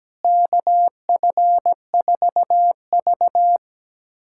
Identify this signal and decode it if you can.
Morse code, heard as a keyed tone, about 17 wpm, 700 Hz. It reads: KF4V
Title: USA Callsign DE Callsign - 4 Characters - 17wpm